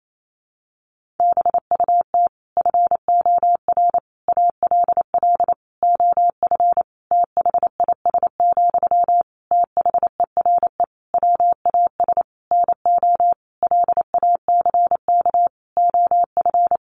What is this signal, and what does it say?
Morse code, heard as a keyed tone, about 28 wpm, 700 Hz. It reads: BUT FOR ALL OF THIS, THERE WAS NO LACK OF
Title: Sentences from Top 1300 Words 28wpm